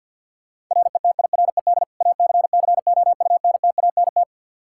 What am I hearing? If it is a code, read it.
Morse code, heard as a keyed tone, about 52 wpm, 700 Hz. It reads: RETIRED ACCOUNTANT
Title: Practice QSO - Exchange 3 - More Info - 50wpm